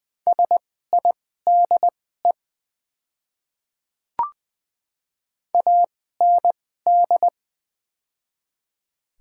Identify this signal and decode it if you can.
Morse code, heard as a keyed tone, about 20 wpm, spaced as for 14 wpm, 700 Hz. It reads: SIDE AND
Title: 2x Character Spacing - Top 200 Words - 20wpm